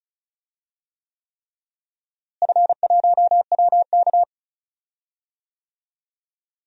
Morse code, heard as a keyed tone, about 36 wpm, 690 Hz. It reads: F1WK
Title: DXCC-100 Call Signs - 4 Characters - Repeated 3x - 35wpm